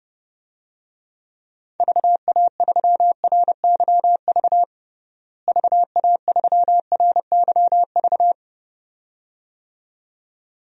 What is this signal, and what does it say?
Morse code, heard as a keyed tone, about 30 wpm, 700 Hz. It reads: VA3RYV VA3RYV
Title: DXCC-100 Call Signs - 6 Characters - Repeated 2x - 30wpm